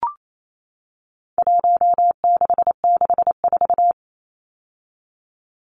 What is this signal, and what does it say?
Morse code, heard as a keyed tone, about 28 wpm, 700 Hz. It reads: 1664
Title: Numbers - Contest Style - 4-Digit 28wpm